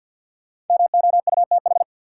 Morse code, heard as a keyed tone, about 47 wpm, 700 Hz. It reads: MOUTH
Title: Sentences from Top 1500 Words - New Word Drill Alt - 50wpm